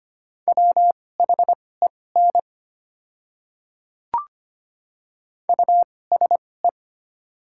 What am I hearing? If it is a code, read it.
Morse code, heard as a keyed tone, about 25 wpm, spaced as for 18 wpm, 690 Hz. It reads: WHEN USE
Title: 2x Character Spacing - Top 200 Words - 25wpm